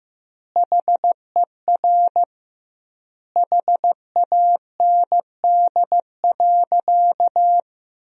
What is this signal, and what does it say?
Morse code, heard as a keyed tone, about 15 wpm, 700 Hz. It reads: HER HAND.
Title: 2x Word-Spacing - Sentences from Top 400 Words - 15wpm